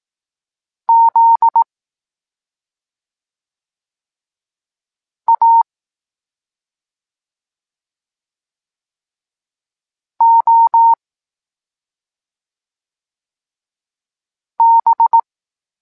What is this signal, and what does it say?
Morse code, heard as a keyed tone, about 18 wpm, 920 Hz. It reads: Z A O B
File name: Tekst-23 4